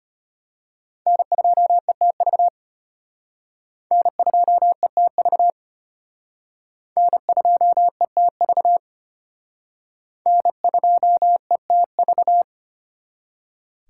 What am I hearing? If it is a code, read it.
Morse code, heard as a keyed tone, about 32 wpm, 700 Hz. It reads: N2ETV N2ETV N2ETV N2ETV